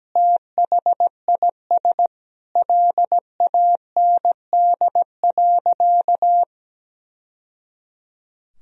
Morse code, heard as a keyed tone, about 17 wpm, 700 Hz. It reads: THIS LAND.
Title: Sentences from Top 1000 Words - Review Part 4 - 17wpm